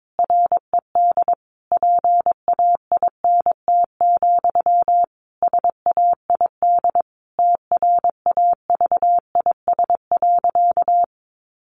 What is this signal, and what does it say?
Morse code, heard as a keyed tone, about 22 wpm, 700 Hz. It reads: RED PAINT, SAID TRAVIS.
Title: A Sound Of Thunder 22wpm